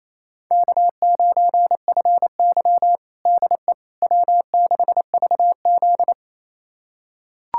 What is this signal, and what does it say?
Morse code, heard as a keyed tone, about 28 wpm, 700 Hz. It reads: K9FY DE W6VZ
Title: USA Callsign DE Callsign - 4 Characters - 28wpm